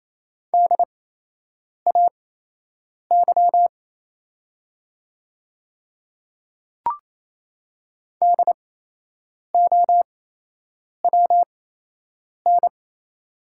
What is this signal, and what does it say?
Morse code, heard as a keyed tone, about 28 wpm, spaced as for 8 wpm, 690 Hz. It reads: DAY DOWN
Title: 8x Character Spacing - Top 100 Words - 28wpm